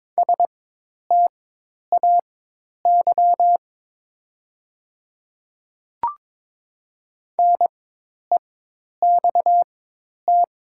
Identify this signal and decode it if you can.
Morse code, heard as a keyed tone, about 22 wpm, spaced as for 10 wpm, 700 Hz. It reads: STAY NEXT
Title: 4x Character Spacing - Top 500 Words - 22wpm